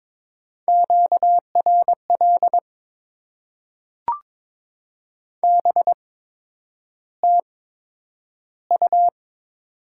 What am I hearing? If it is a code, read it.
Morse code, heard as a keyed tone, about 22 wpm, spaced as for 9 wpm, 700 Hz. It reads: QRL BTU
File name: Top 20 QSO Elements - 8x Character Spacing - 22wpm